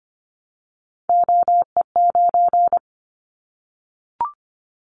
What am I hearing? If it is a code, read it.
Morse code, heard as a keyed tone, about 25 wpm, 700 Hz. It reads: OE9
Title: Callsign Prefixes for Top 100 DXCC - 25wpm